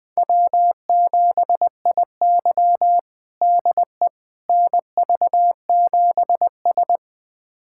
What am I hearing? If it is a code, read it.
Morse code, heard as a keyed tone, about 20 wpm, 700 Hz. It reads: W7IY DE NV7S
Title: USA Callsign DE Callsign - 4 Characters - 20wpm